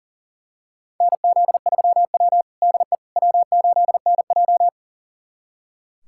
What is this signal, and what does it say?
Morse code, heard as a keyed tone, about 37 wpm, 700 Hz. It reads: NZ3W DE W8NJ
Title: USA Callsign DE Callsign - 4 Characters - 40wpm